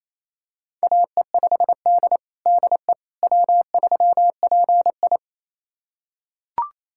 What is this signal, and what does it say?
Morse code, heard as a keyed tone, about 28 wpm, 700 Hz. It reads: AE5D DE W3PI
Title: USA Callsign DE Callsign - 4 Characters - 28wpm